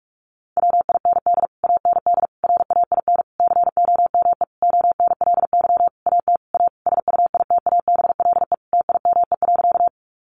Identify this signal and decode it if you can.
Morse code, heard as a keyed tone, about 44 wpm, 700 Hz. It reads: WIND AND RAIN COME ONLY AT A SUITABLE TIME.
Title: Sentences from Top 1400 Words 45wpm